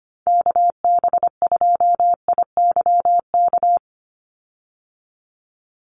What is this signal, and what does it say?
Morse code, heard as a keyed tone, about 25 wpm, 700 Hz. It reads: KB2IYK
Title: USA Call Signs 25wpm